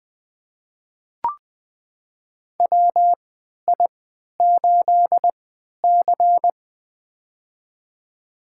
Morse code, heard as a keyed tone, about 20 wpm, spaced as for 11 wpm, 700 Hz. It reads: WI8C
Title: USA Call Signs - 4 Characters - 3x Character Spacing - 20wpm